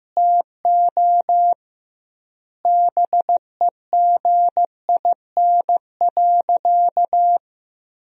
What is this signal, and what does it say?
Morse code, heard as a keyed tone, about 15 wpm, 700 Hz. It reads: TO BEGIN.
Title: 2x Word-Spacing - Sentences from Top 300 Words - 15wpm